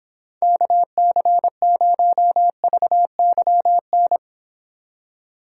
Morse code, heard as a keyed tone, about 30 wpm, 700 Hz. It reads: KC0VYN